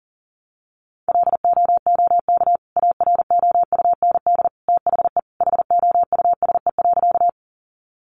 Morse code, heard as a keyed tone, about 40 wpm, 700 Hz. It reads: LOOK AROUND THE HOUSE.
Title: Sentences from Top 600 Words 40wpm